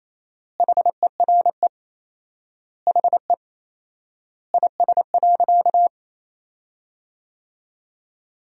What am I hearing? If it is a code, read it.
Morse code, heard as a keyed tone, about 28 wpm, 700 Hz. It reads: HERE HE IS.
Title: 4x Word-Spacing - Sentences from Top 200 Words - 28wpm